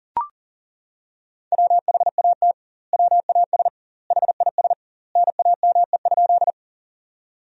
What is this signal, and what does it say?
Morse code, heard as a keyed tone, about 40 wpm, 700 Hz. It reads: WHAT WAS HIS NAME?
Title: 2x Word-Spacing - Sentences from Top 200 Words - 40wpm